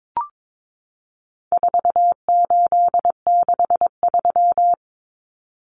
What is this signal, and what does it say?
Morse code, heard as a keyed tone, about 22 wpm, 700 Hz. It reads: 4863
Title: Common Cut Numbers - 4-Digit 22wpm